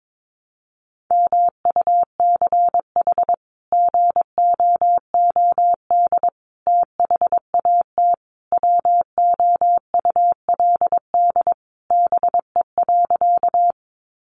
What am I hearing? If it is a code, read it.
Morse code, heard as a keyed tone, about 22 wpm, 700 Hz. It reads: MUCH GOOD THAT WOULD BE.
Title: Sentences from Top 200 Words 22wpm